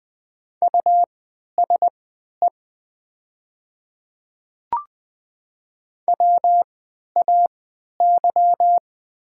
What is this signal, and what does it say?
Morse code, heard as a keyed tone, about 20 wpm, spaced as for 11 wpm, 700 Hz. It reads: USE WAY
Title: 3x Character Spacing - Top 100 Words - 20wpm